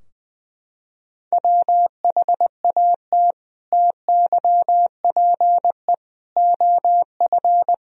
Morse code, heard as a keyed tone, about 21 wpm, 700 Hz. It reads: WHAT TYPE OF
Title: Sentences from Top 1000 Words - Review Part 3 - 20wpm